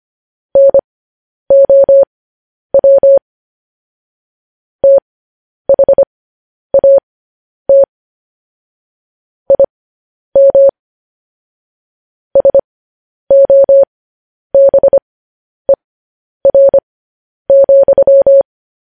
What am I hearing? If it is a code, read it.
Morse code, heard as a keyed tone, about 25 wpm, spaced as for 10 wpm, 550 Hz. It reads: NOW THAT IM SOBER,
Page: news headlines in morse code at 10 wpm